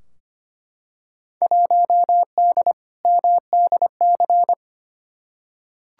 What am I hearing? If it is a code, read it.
Morse code, heard as a keyed tone, about 25 wpm, 700 Hz. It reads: 1D MDC
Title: Field Day Exchanges - 25wpm